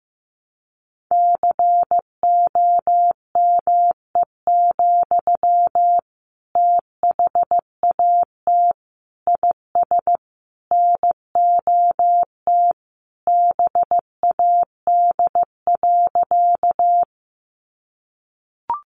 Morse code, heard as a keyed tone, about 14 wpm, 700 Hz. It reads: COME, THAT IS NOT BAD.
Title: Sentences from Top 700 Words 15wpm